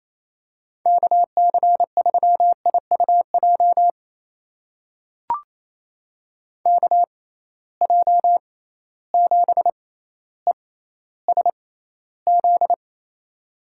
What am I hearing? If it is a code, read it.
Morse code, heard as a keyed tone, about 27 wpm, spaced as for 14 wpm, 700 Hz. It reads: KC3IUJ KJ7ESZ
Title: USA Call Signs - 6 Characters - 6x Character Spacing - 28wpm